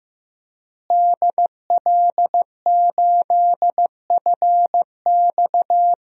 Morse code, heard as a keyed tone, about 15 wpm, 700 Hz. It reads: DL8FX
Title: DXCC-100 Call Signs - 5 Characters - Repeated 3x - 15wpm